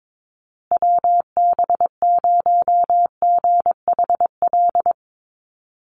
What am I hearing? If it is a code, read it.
Morse code, heard as a keyed tone, about 22 wpm, 700 Hz. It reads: WB0GHL
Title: USA Call Signs Repeated 2x 22wpm